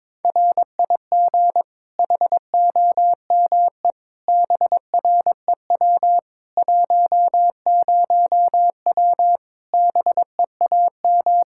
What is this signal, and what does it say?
Morse code, heard as a keyed tone, about 21 wpm, 700 Hz. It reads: RIG HOME BREW 10W BEAM
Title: Practice QSO - Exchange 2 - Info - 22wpm